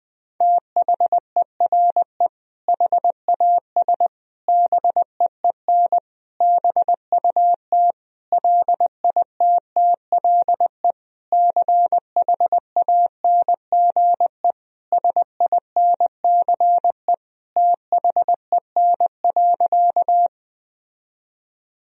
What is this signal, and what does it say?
Morse code, heard as a keyed tone, about 20 wpm, 700 Hz. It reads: THERE HAS BEEN BUT LITTLE CHANGE SINCE THEN.
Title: Sentences from Top 1000 Words - Review Part 1 - 20wpm